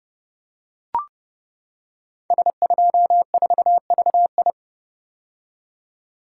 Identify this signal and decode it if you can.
Morse code, heard as a keyed tone, about 30 wpm, 700 Hz. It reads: S24VI
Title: Code Groups - 5 Letter:Numbers - 30wpm